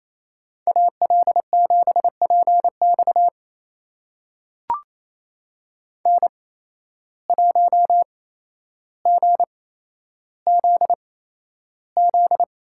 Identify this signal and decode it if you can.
Morse code, heard as a keyed tone, about 28 wpm, spaced as for 12 wpm, 700 Hz. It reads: AL7PX N1GZZ
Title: USA Call Signs - 5 Characters - 8x Character Spacing - 28wpm